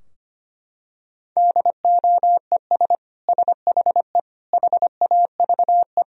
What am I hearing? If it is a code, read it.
Morse code, heard as a keyed tone, about 25 wpm, 690 Hz. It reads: DOES SHE HAVE
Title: Sentences from Top 1000 Words - Review Part 4 - 25wpm